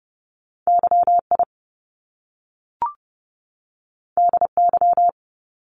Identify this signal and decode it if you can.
Morse code, heard as a keyed tone, about 30 wpm, 700 Hz. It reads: YI DY
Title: Top Two Letter Combinations in Words 30wpm